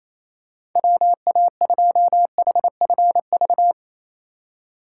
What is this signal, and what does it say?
Morse code, heard as a keyed tone, about 28 wpm, 700 Hz. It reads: WA2HFV
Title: USA Call Signs - 2x Character Spacing - 28wpm